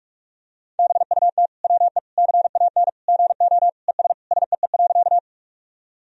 Morse code, heard as a keyed tone, about 46 wpm, 700 Hz. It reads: BUT WE CAN GO ES SEE.
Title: Sentences from Top 100 Words and Abbreviations 45wpm